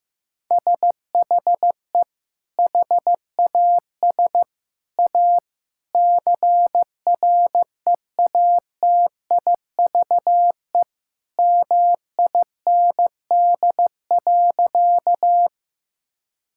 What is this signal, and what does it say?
Morse code, heard as a keyed tone, about 15 wpm, 700 Hz. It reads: SHE HAS A CREATIVE MIND.
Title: Sentences from Top 1000 Words - Review Part 5 - 15wpm